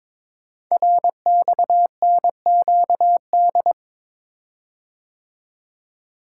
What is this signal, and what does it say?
Morse code, heard as a keyed tone, about 22 wpm, 700 Hz. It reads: RXNQD
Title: Code Groups - 5 Letters - 22wpm